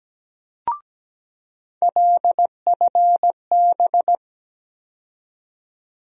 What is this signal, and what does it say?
Morse code, heard as a keyed tone, about 17 wpm, 700 Hz. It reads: LFB
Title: USA Call Sign Postfixes - 3 Letters 17wpm